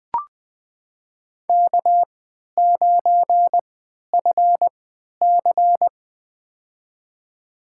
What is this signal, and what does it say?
Morse code, heard as a keyed tone, about 21 wpm, spaced as for 12 wpm, 700 Hz. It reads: K9FC
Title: USA Call Signs - 4 Characters - 3x Character Spacing - 20wpm